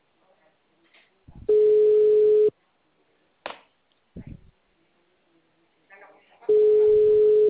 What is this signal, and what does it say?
Ringback tone (Europe, 425 Hz), with speech.